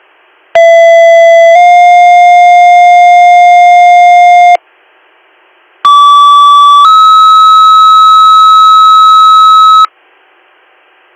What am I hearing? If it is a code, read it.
Two-tone page, heard as a one-second tone, then a three-second tone, about 670 then 710 Hz, about 1150 then 1290 Hz. It is heard from 0.5 to 9.9 s.